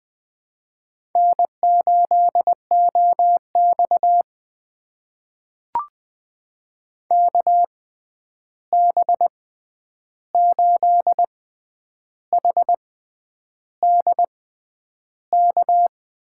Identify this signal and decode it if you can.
Morse code, heard as a keyed tone, about 20 wpm, spaced as for 9 wpm, 700 Hz. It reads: N8OX KB8HDK
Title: USA Call Signs - 6x Character Spacing - 20wpm